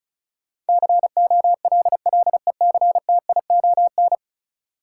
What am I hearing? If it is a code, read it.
Morse code, heard as a keyed tone, about 36 wpm, 700 Hz. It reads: COLLECTION
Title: Sentences from Top 1200 Words - New Word Drill - 35wpm